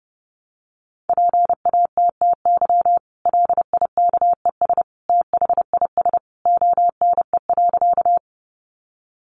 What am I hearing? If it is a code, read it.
Morse code, heard as a keyed tone, about 28 wpm, 700 Hz. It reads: PATTY LIKES THIS ONE.
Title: Additional Easy Phrases 30wpm